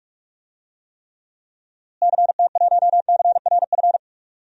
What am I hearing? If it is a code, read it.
Morse code, heard as a keyed tone, about 46 wpm, 700 Hz. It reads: CT1CRF
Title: DXCC-100 Call Signs - 6 Characters - Repeated 2x - 45wpm